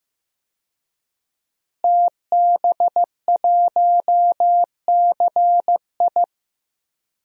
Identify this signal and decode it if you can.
Morse code, heard as a keyed tone, about 16 wpm, 700 Hz. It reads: TB1CI
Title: DXCC-100 Call Signs - 5 Characters - Repeated 3x - 15wpm